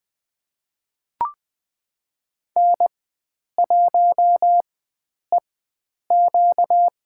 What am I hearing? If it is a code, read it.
Morse code, heard as a keyed tone, about 20 wpm, spaced as for 9 wpm, 700 Hz. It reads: N1EQ
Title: USA Call Signs - 4 Characters - 4x Character Spacing - 20wpm